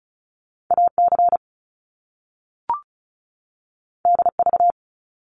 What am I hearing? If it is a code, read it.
Morse code, heard as a keyed tone, about 36 wpm, 700 Hz. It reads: AC DV
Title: Top Two Letter Combinations in Words 35wpm